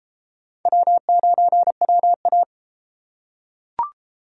Morse code, heard as a keyed tone, about 32 wpm, 700 Hz. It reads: W9WA